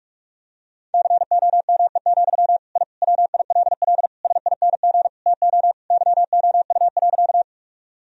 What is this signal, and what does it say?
Morse code, heard as a keyed tone, about 46 wpm, 700 Hz. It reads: COME, I WILL SING TO YOU.
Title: Sentences from Top 1000 Words - Review Part 2 - 45wpm